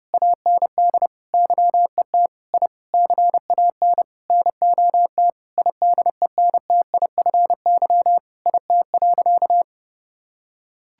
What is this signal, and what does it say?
Morse code, heard as a keyed tone, about 30 wpm, 700 Hz. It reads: AND YET I CAN NOT IDENTIFY IT.
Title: Sentences from Top 1000 Words - Review Part 6 - 30wpm